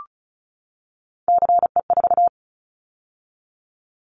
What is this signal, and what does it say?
Morse code, heard as a keyed tone, about 36 wpm, 700 Hz. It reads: CE4
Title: Callsign Prefixes for Top 100 DXCC - 35wpm